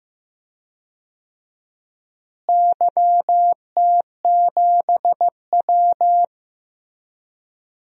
Morse code, heard as a keyed tone, about 15 wpm, 700 Hz. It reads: YT7W
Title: DXCC-100 Call Signs - 4 Characters - Repeated 2x - 15wpm